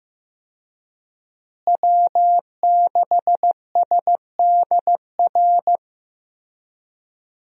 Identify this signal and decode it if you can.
Morse code, heard as a keyed tone, about 15 wpm, 700 Hz. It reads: W6SDR
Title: USA Call Signs - 5 Characters - Repeated 2x - 15wpm